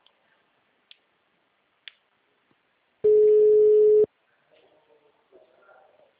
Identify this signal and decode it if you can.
Ringback tone (Europe, 425 Hz).